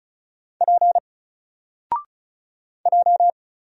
Morse code, heard as a keyed tone, about 35 wpm, 700 Hz. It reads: P J
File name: Single-Letter-Number-35wpm